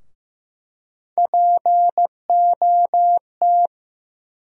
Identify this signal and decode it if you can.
Morse code, heard as a keyed tone, about 15 wpm, 700 Hz. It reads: POT
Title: USA Call Sign Postfixes - 3 Letters 15wpm